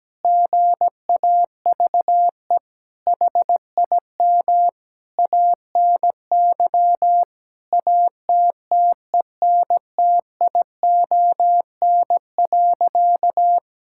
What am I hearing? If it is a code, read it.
Morse code, heard as a keyed tone, about 17 wpm, 700 Hz. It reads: GAVE HIM ANY ATTENTION.
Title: Sentences from Top 1000 Words - Review Part 6 - 17wpm